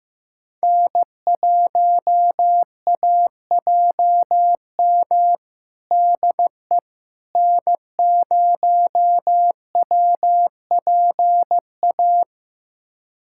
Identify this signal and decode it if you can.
Morse code, heard as a keyed tone, about 16 wpm, 700 Hz. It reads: N1AJM DE N0WPA
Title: USA Callsign DE Callsign 15wpm